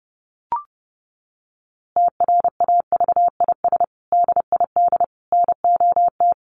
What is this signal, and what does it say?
Morse code, heard as a keyed tone, about 30 wpm, 700 Hz. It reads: TRAVIS DID NOT
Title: A Sound Of Thunder 30wpm